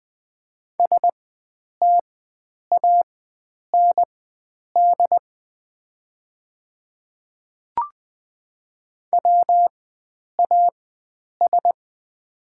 Morse code, heard as a keyed tone, about 20 wpm, spaced as for 9 wpm, 690 Hz. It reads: STAND WAS
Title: 4x Character Spacing - Top 300 Words - 20wpm